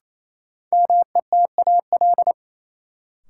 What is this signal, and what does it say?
Morse code, heard as a keyed tone, about 28 wpm, 700 Hz. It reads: METAL
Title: Sentences from Top 1000 Words - Review Part 6 - New Word Drill - 28wpm